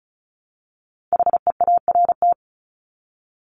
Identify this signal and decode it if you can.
Morse code, heard as a keyed tone, about 34 wpm, 700 Hz. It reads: HEART
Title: Top 1001-1250 Words 35wpm